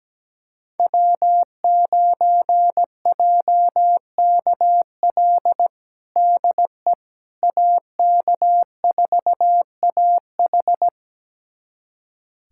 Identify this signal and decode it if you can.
Morse code, heard as a keyed tone, about 17 wpm, 700 Hz. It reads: W9JKL DE AK4AH
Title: USA Callsign DE Callsign 17wpm